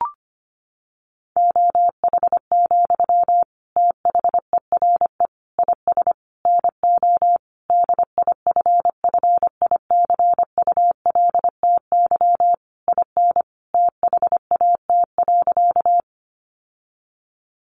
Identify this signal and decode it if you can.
Morse code, heard as a keyed tone, about 25 wpm, 700 Hz. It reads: OH, THERE IS NO DIFFICULTY IN THAT.
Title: Sentences from Top 1500 Words 25wpm